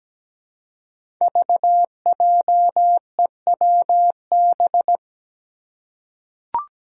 Morse code, heard as a keyed tone, about 17 wpm, 700 Hz. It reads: VJEWB
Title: Five Challenging Letter-Number Combinations 17wpm